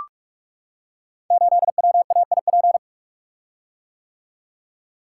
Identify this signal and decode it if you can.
Morse code, heard as a keyed tone, about 44 wpm, 700 Hz. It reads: 8WAIP
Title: Code Groups - 5 Letter:Numbers - 45wpm